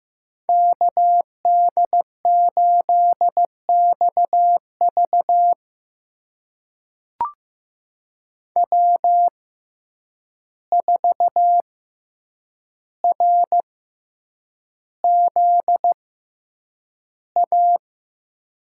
Morse code, heard as a keyed tone, about 15 wpm, spaced as for 8 wpm, 700 Hz. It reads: KD8XV W4RZA
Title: USA Call Signs - 5 Characters - 6x Character Spacing - 15wpm